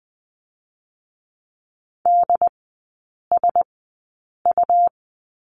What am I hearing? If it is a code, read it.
Morse code, heard as a keyed tone, about 20 wpm, spaced as for 8 wpm, 700 Hz. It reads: DSU
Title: 184 Lesson 30e - Letter b - Three Characters 20wpm